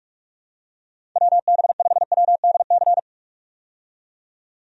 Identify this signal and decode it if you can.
Morse code, heard as a keyed tone, about 46 wpm, 700 Hz. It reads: WB5WDC
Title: USA Call Signs - 6 Characters - 45wpm